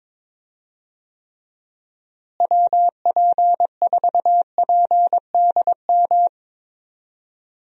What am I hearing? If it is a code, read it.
Morse code, heard as a keyed tone, about 22 wpm, 700 Hz. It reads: WP4PDM
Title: USA Call Signs - 6 Characters - Repeated 3x - 22wpm